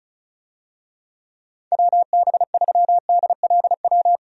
Morse code, heard as a keyed tone, about 34 wpm, 700 Hz. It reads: WB3DLW
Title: USA Call Signs - 6 Characters - 35wpm